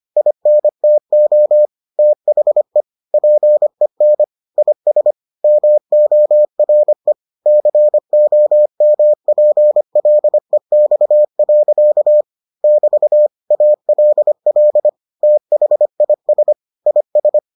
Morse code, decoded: INTO THE PEN IS MORE COMPLEX. <BT> ALL THIS IS

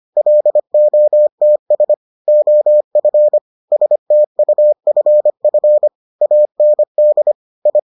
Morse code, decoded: LOTS OF STUFF AND I

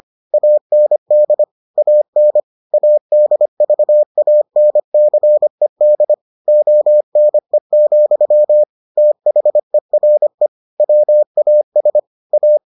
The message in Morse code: AND AN ADVANCED ONE, THERE WAS A

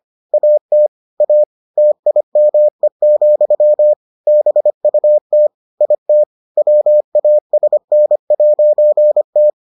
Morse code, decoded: AT A TIME, BUT IT WASN'T

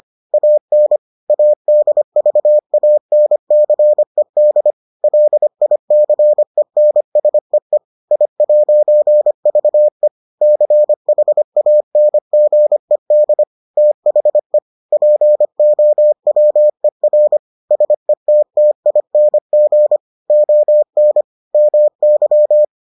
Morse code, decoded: AN ADVANCED LICENSEE I'VE CHANGED THE POWER SETTING ON MY

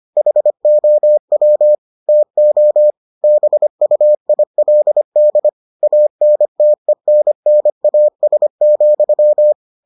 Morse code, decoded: HOW TO BUILD ANTENNAS,

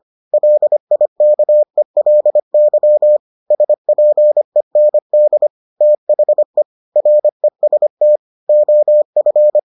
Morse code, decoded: LIKELY SPEND THE REST OF

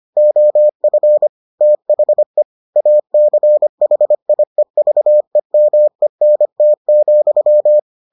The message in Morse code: OF THE ACHIEVEMENT,

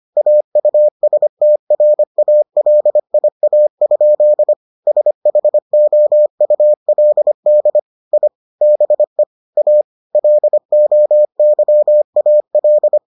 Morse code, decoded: AUSTRALIA? SHOULD I BE A LOYAL

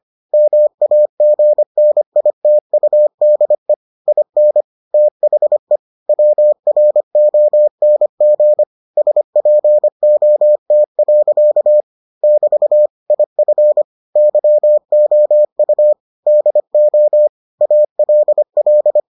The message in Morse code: MAGNITUDE IN THE WRONG SPOT. <BT> IF YOU DO ALL

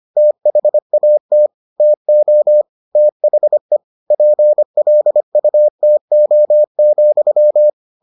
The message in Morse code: THAT TO THE PLUTO,

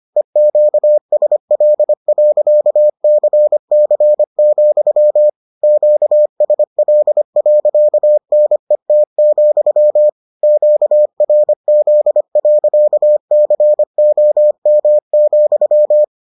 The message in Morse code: EQSL.CC, QSL.NET, QRZ.COM,

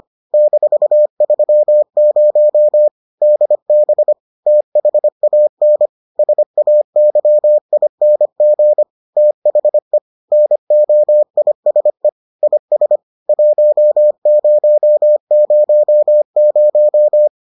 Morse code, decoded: -30 DB THAN SAYING THE NOISE IS 1000

